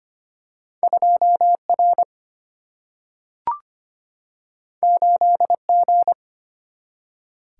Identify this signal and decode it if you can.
Morse code, heard as a keyed tone, about 25 wpm, 700 Hz. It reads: 2 R 8 G
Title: Single-Letter Single-Number 25wpm